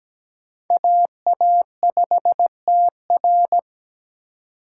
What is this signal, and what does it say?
Morse code, heard as a keyed tone, about 17 wpm, 700 Hz. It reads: AA5TR
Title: USA Call Signs - 5 Characters - 3x Character Spacing - 17wpm